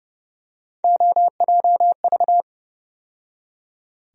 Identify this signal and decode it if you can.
Morse code, heard as a keyed tone, about 30 wpm, 700 Hz. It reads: OJV
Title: USA Call Sign Postfixes - 3 Letters 30wpm